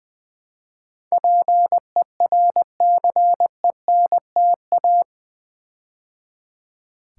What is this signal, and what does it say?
Morse code, heard as a keyed tone, about 20 wpm, 700 Hz. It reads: PERCENTA
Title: Build A Word - 4 Letters At A Time - 10-Letter Words - 20wpm